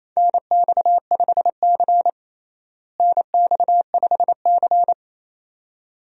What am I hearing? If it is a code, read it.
Morse code, heard as a keyed tone, about 28 wpm, 700 Hz. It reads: NX5C NX5C
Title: USA Call Signs - 4 Characters - Repeated 2x - 28wpm